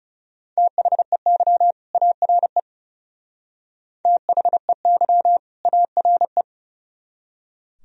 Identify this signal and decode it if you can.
Morse code, heard as a keyed tone, about 35 wpm, 700 Hz. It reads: THEY ARE THEY ARE
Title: Speed-Racing - Sets of 2 Words 30wpm